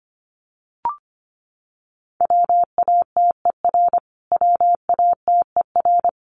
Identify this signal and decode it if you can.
Morse code, heard as a keyed tone, about 25 wpm, 700 Hz. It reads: WATER WATER
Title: Top 300 Words Repeated 25wpm